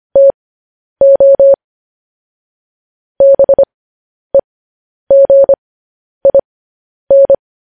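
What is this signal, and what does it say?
Morse code, heard as a keyed tone, about 25 wpm, spaced as for 10 wpm, 550 Hz. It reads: TO BEGIN
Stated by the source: Morse code news headlines practice at 10 wpm.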